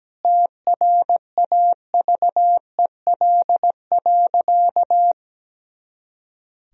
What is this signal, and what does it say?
Morse code, heard as a keyed tone, about 17 wpm, 700 Hz. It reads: TRAVEL.
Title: Sentences from Top 400 Words and Abbreviations 17wpm